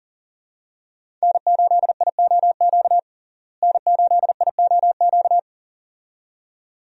Morse code, heard as a keyed tone, about 40 wpm, 700 Hz. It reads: N8IOQ N8IOQ
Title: USA Call Signs - 5 Characters - Repeated 2x - 40wpm